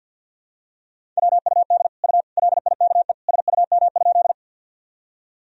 Morse code, heard as a keyed tone, about 52 wpm, 700 Hz. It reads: WUD U LIKE SUM?
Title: Sentences from Top 100 Words and Abbreviations 50wpm